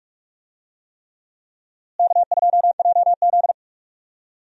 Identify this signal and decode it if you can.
Morse code, heard as a keyed tone, about 44 wpm, 700 Hz. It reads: K2JZ
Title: USA Call Signs - 4 Characters - Repeated 3x - 45wpm